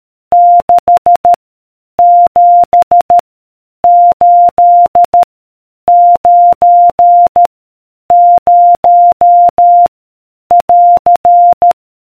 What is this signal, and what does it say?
Morse code, heard as a keyed tone, about 13 wpm, 700 Hz. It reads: 6 7 8 9 0 <AR>